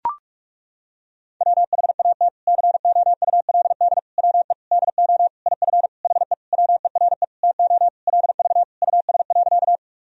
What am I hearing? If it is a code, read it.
Morse code, heard as a keyed tone, about 42 wpm, 700 Hz. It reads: WHAT COULD WE DO IF HE WERE TO LV US.
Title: Sentences from Top 400 Words and Abbreviations 45wpm